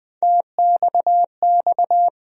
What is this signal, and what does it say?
Morse code, heard as a keyed tone, about 20 wpm, 700 Hz. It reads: TXX
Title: USA Call Sign Postfixes - 3 Letters 20wpm